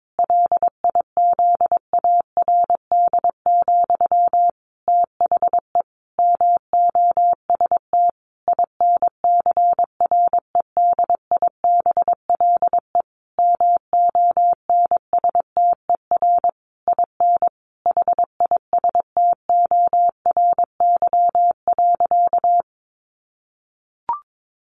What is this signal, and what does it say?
Morse code, heard as a keyed tone, about 22 wpm, 700 Hz. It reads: LIZARD, THE MOST INCREDIBLE MONSTER IN HISTORY.
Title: A Sound Of Thunder 22wpm